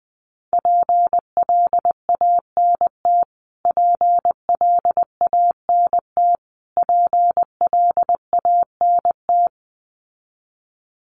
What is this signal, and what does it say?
Morse code, heard as a keyed tone, about 20 wpm, 700 Hz. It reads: PLANT PLANT PLANT
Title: Top 300 Words Repeated 20wpm